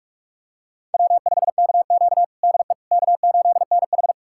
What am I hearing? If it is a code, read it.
Morse code, heard as a keyed tone, about 46 wpm, 700 Hz. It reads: W5KQ DE K8NH
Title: USA Callsign DE Callsign - 4 Characters - 45wpm